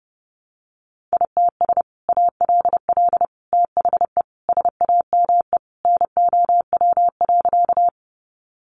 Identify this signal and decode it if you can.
Morse code, heard as a keyed tone, about 30 wpm, 700 Hz. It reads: ITS ALL THE SAME NOW.
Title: Sentences from Top 600 Words 30wpm